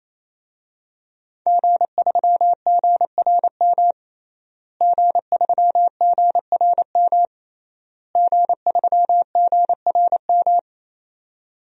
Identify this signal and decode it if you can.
Morse code, heard as a keyed tone, about 28 wpm, 700 Hz. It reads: G3GRM G3GRM G3GRM
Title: DXCC-100 Call Signs - 5 Characters - Repeated 3x - 28wpm